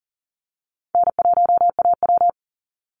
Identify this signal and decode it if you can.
Morse code, heard as a keyed tone, about 40 wpm, 700 Hz. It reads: N1AW
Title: NA SOTA Chaser Callsigns 40wpm